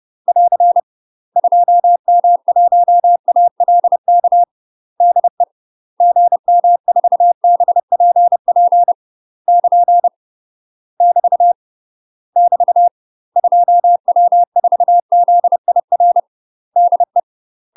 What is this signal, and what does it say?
Morse code, heard as a keyed tone, about 30 wpm, 700 Hz. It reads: <AR> 2M1ALK DE GM4BPP <KN> <BT> <BT> 2W4ZIR DE
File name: UK MASTER QSOs TRAINING/30WPM